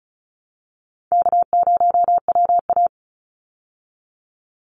Morse code, heard as a keyed tone, about 34 wpm, 700 Hz. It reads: K0WA
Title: Call Signs Repeated 3x - 35wpm